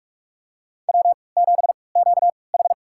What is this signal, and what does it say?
Morse code, heard as a keyed tone, about 47 wpm, spaced as for 26 wpm, 700 Hz. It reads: W7QH
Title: USA Call Signs - 3x Character Spacing - 45wpm